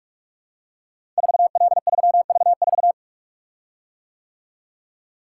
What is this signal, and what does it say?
Morse code, heard as a keyed tone, about 44 wpm, 690 Hz. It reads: 4L3V4
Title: Code Groups - 5 Letter:Numbers - 45wpm